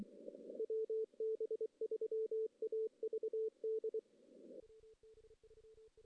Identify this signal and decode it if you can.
Morse code, heard as a keyed tone, about 24 wpm, 440 Hz. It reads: WB3AVD
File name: 2016 JAN CW NAQP / 2016 Jan naqp k5ot - 0-2